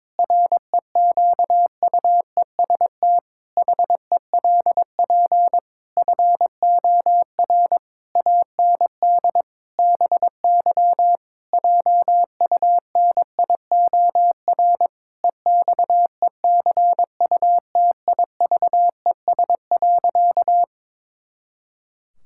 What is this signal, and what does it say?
Morse code, heard as a keyed tone, about 22 wpm, 700 Hz. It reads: REQUEST HELP FOR AND BY JUNIOR EXECUTIVES.
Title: Pangram Challenge v2 - 22wpm